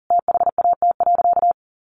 Morse code, heard as a keyed tone, about 40 wpm, 700 Hz. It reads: THAT.
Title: Sentences from Top 1000 Words 40wpm